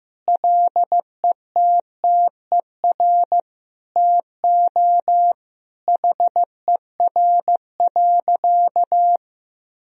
Morse code, heard as a keyed tone, about 16 wpm, 700 Hz. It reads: LETTER TO HER.
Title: Sentences from Top 1000 Words - Review Part 3 - 15wpm